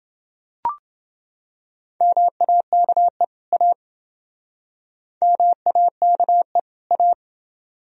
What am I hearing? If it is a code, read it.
Morse code, heard as a keyed tone, about 28 wpm, 700 Hz. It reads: MAKE A MAKE A